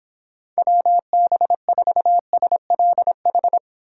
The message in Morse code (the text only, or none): WB4SLH